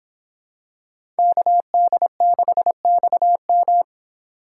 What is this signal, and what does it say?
Morse code, heard as a keyed tone, about 26 wpm, 700 Hz. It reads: KD6XM